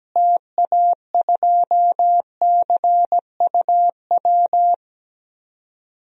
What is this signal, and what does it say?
Morse code, heard as a keyed tone, about 17 wpm, 700 Hz. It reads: TA2CUW
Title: DXCC-100 Call Signs - 6 Characters - Repeated 2x - 17wpm